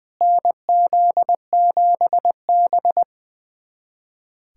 Morse code, heard as a keyed tone, about 21 wpm, 700 Hz. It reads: NZ7B
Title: USA Callsign DE Callsign - 4 Characters - 20wpm